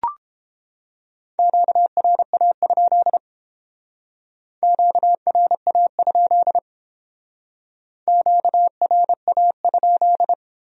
Morse code, heard as a keyed tone, about 28 wpm, 700 Hz. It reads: QRA? QRA? QRA?